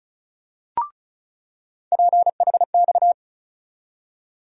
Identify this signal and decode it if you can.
Morse code, heard as a keyed tone, about 34 wpm, 700 Hz. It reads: PHX
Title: USA Call Sign Postfixes - 3 Letters 35wpm